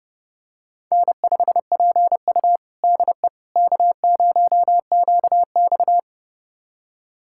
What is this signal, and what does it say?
Morse code, heard as a keyed tone, about 30 wpm, 700 Hz. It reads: N5PU DE K0QX
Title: USA Callsign DE Callsign - 4 Characters - 30wpm